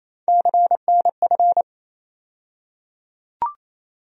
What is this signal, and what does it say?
Morse code, heard as a keyed tone, about 28 wpm, 700 Hz. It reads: CNF
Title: USA Call Sign Postfixes - 3 Letters 28wpm